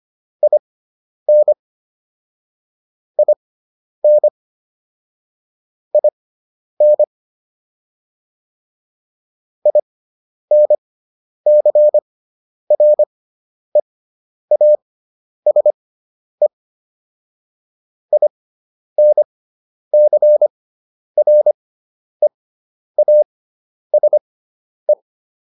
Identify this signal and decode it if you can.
Morse code, heard as a keyed tone, about 25 wpm, spaced as for 10 wpm, 600 Hz. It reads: IN IN IN INCREASE INCREASE